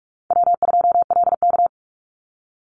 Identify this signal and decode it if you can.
Morse code, heard as a keyed tone, about 46 wpm, 700 Hz. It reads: W2LK
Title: Easy Call Signs - 45wpm